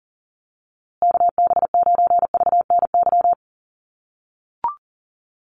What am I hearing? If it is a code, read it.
Morse code, heard as a keyed tone, about 40 wpm, 700 Hz. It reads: KB9VNY
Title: USA Call Signs Repeated 3x 40wpm